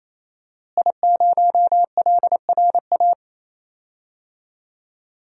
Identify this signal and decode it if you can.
Morse code, heard as a keyed tone, about 28 wpm, 690 Hz. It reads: I0LRA
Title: DXCC-100 Call Signs - 5 Characters - Repeated 3x - 28wpm